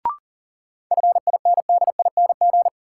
Morse code, heard as a keyed tone, about 40 wpm, 700 Hz. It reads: FINDING
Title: Sentences from Top 1100 Words - New Word Drill Alt - 40wpm